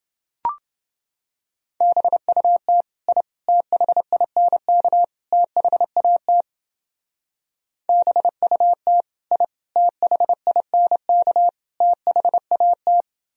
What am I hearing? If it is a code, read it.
Morse code, heard as a keyed tone, about 29 wpm, 700 Hz. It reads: BUT I THINK THAT BUT I THINK THAT